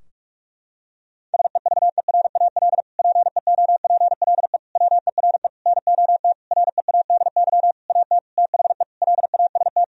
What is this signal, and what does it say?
Morse code, heard as a keyed tone, about 41 wpm, 700 Hz. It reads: SEVERAL PEOPLE WERE NOT READY AT THE LAST
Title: Sentences from Top 1000 Words - Review Part 2 - 45wpm